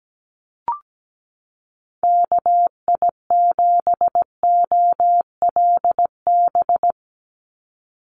Morse code, heard as a keyed tone, about 17 wpm, 700 Hz. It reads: KI7OLB
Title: USA Call Signs 17wpm